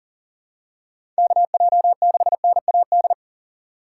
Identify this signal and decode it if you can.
Morse code, heard as a keyed tone, about 40 wpm, 700 Hz. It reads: KJ6NAD
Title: USA Call Signs - 4x Character Spacing - 40wpm